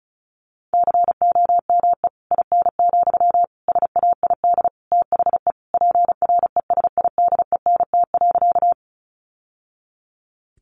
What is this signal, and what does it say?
Morse code, heard as a keyed tone, about 34 wpm, 700 Hz. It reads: COME IN, SAID THE PRESIDENT.
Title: Sentences from Top 1000 Words 35wpm